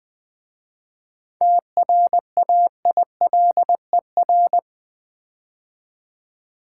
Morse code, heard as a keyed tone, about 20 wpm, 700 Hz. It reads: TRAILER
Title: 7 Letter Words - First 3 Letters Given - 20wpm